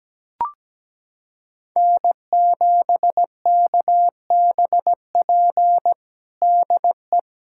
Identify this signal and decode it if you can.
Morse code, heard as a keyed tone, about 17 wpm, 700 Hz. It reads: N7KBP DE
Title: USA Callsign DE Callsign 17wpm